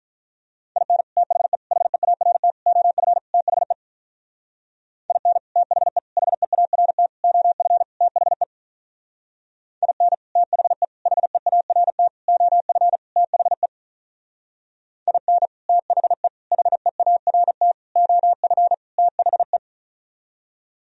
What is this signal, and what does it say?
Morse code, heard as a keyed tone, about 46 wpm, 700 Hz. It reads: IN THE HEART OF THE IN THE HEART OF THE IN THE HEART OF THE IN THE HEART OF THE